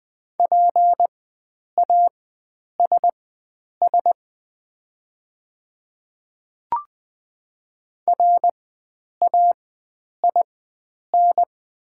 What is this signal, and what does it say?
Morse code, heard as a keyed tone, about 20 wpm, spaced as for 9 wpm, 700 Hz. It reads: PASS RAIN
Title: 4x Character Spacing - Top 500 Words - 20wpm